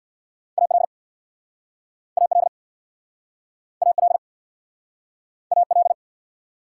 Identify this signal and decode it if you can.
Morse code, heard as a keyed tone, about 66 wpm, 690 Hz. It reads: AL AL AL AL